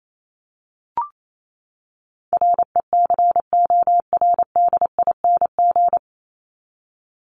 Morse code, heard as a keyed tone, about 28 wpm, 700 Hz. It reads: RECORDING
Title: Top 1501-1750 Words 28wpm